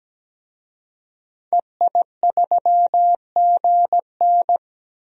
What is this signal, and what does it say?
Morse code, heard as a keyed tone, about 17 wpm, 700 Hz. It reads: EI3GN
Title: DXCC-100 Call Signs - 5 Characters - 17wpm